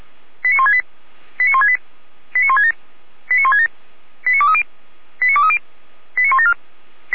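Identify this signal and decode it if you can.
Five-tone selective call (ZVEI), heard as quick runs of five tones, tones of about 80 ms. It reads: 89168 89168 89167 89167 89230 89230 89174
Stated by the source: ZVEI five-tone selective call received by SDR.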